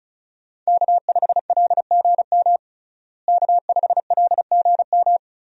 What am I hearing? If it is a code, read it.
Morse code, heard as a keyed tone, about 34 wpm, 700 Hz. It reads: K5LGM K5LGM
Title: USA Call Signs - 5 Characters - Repeated 2x - 35wpm